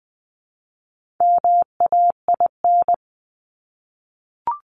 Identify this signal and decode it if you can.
Morse code, heard as a keyed tone, about 20 wpm, 700 Hz. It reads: MAIN
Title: Top 300 Words 20wpm